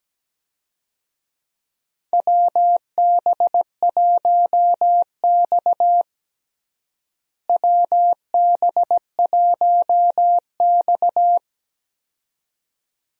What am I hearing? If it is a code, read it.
Morse code, heard as a keyed tone, about 17 wpm, 700 Hz. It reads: WB1X WB1X
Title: USA Call Signs - 4 Characters - Repeated 2x - 17wpm